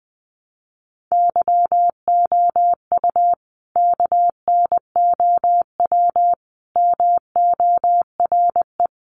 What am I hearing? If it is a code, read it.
Morse code, heard as a keyed tone, about 20 wpm, 700 Hz. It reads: YOU KNOW MORE
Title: Sentences from Top 100 Words 20wpm